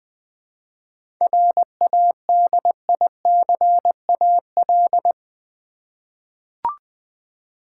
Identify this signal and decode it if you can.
Morse code, heard as a keyed tone, about 20 wpm, 700 Hz. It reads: RADICAL
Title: 7 Letter Words - First 3 Letters Given - 20wpm